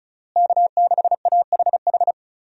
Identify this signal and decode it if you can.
Morse code, heard as a keyed tone, about 34 wpm, 700 Hz. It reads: K6AHH
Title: USA Call Signs - 5 Characters - Repeated 3x - 35wpm